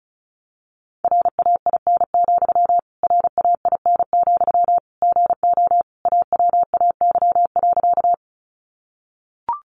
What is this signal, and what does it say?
Morse code, heard as a keyed tone, about 34 wpm, 700 Hz. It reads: RAIN, RAIN, GO AWAY.
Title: Sentences from Top 600 Words 35wpm